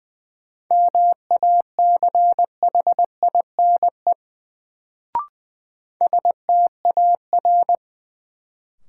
Morse code, heard as a keyed tone, about 20 wpm, 700 Hz. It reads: MACHINE STAR
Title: Sentences from Top 1000 Words - Review Part 2 - New Word Drill - 20wpm